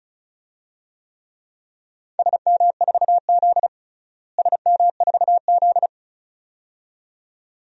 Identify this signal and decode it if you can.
Morse code, heard as a keyed tone, about 34 wpm, 700 Hz. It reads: SM4Z SM4Z
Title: DXCC-100 Call Signs - 4 Characters - Repeated 2x - 35wpm